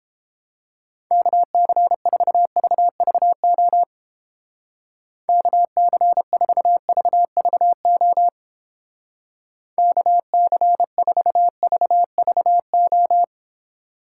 Morse code, heard as a keyed tone, about 29 wpm, 700 Hz. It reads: KC4VVO KC4VVO KC4VVO